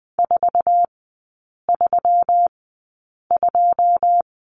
Morse code, heard as a keyed tone, about 20 wpm, spaced as for 8 wpm, 700 Hz. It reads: 432
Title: 133 Lesson 22e - Question Mark - Three Characters 20wpm